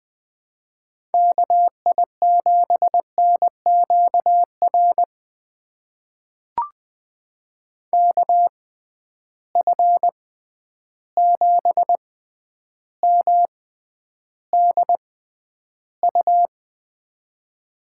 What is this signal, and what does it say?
Morse code, heard as a keyed tone, about 20 wpm, spaced as for 10 wpm, 700 Hz. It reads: KI7NQR KF7MDU
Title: USA Call Signs - 6 Characters - 6x Character Spacing - 20wpm